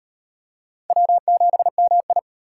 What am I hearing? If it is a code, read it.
Morse code, heard as a keyed tone, about 38 wpm, 700 Hz. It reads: W7MI